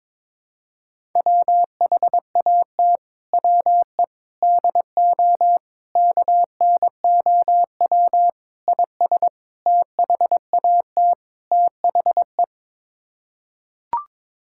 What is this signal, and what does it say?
Morse code, heard as a keyed tone, about 22 wpm, 700 Hz. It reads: WHAT WE DO KNOW IS THAT THE
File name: practice / 22wpm / Sets of 7 Words - 22wpm